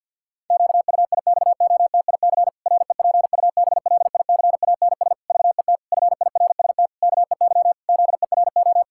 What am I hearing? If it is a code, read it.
Morse code, heard as a keyed tone, about 52 wpm, 700 Hz. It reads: QUIXOTIC REPUBLICANS VET FIRST KEY ZERO
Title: Pangram Challenge v2 - 50wpm